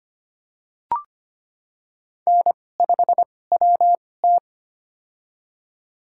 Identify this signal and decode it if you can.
Morse code, heard as a keyed tone, about 25 wpm, spaced as for 18 wpm, 700 Hz. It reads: N5WT
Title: USA Call Signs - 4 Characters - 2x Character Spacing - 25wpm